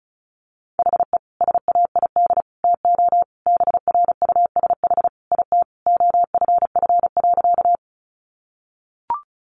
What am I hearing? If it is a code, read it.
Morse code, heard as a keyed tone, about 34 wpm, 700 Hz. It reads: HE SAID TO BRUSH IT OFF.
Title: Sentences from Top 1100 Words 35wpm